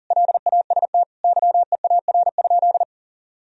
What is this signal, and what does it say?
Morse code, heard as a keyed tone, about 40 wpm, 700 Hz. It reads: LAST YEAR?
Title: Sentences from Top 1000 Words - Review Part 4 - 40wpm